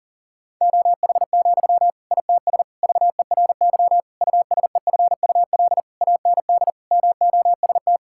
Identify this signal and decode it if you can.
Morse code, heard as a keyed tone, about 40 wpm, 700 Hz. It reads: OH, ITS VERY USEFUL AND MOST
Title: Sentences from Top 1000 Words - Review Part 6 - 40wpm